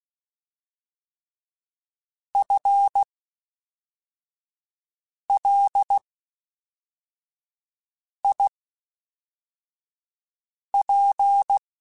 Morse code, heard as a keyed tone, about 16 wpm, spaced as for 4 wpm, 800 Hz. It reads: FLIP
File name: Spaced / FL Runs sp mp3s / FL Runs sp @5